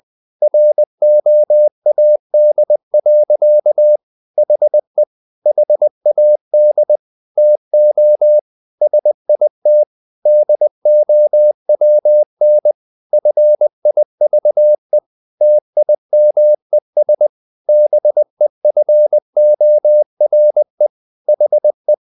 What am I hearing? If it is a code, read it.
Morse code, heard as a keyed tone, about 20 wpm, 600 Hz. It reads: ROAD. HE HAD TO SIT DOWN FIVE TIMES BEFORE HE
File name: the old man and the sea part 2 20wpm